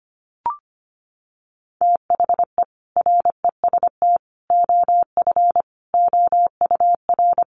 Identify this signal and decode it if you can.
Morse code, heard as a keyed tone, about 25 wpm, 700 Hz. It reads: THE REST OF OUR
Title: Sentences from Top 1400 Words 25wpm